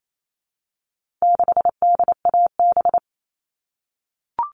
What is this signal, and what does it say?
Morse code, heard as a keyed tone, about 28 wpm, 700 Hz. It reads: 6DAB
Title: All Cut Numbers - 4-Digit 28wpm